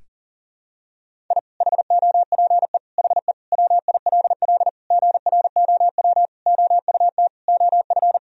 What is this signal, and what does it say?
Morse code, heard as a keyed tone, about 40 wpm, 700 Hz. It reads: I HOPE HE WILL GROW OUT OF
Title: Sentences from Top 1000 Words - Review Part 1 - 40wpm